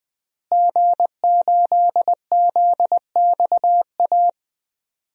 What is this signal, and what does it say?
Morse code, heard as a keyed tone, about 21 wpm, 700 Hz. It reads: G8ZXA
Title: DXCC-100 Call Signs - 5 Characters - Repeated 3x - 20wpm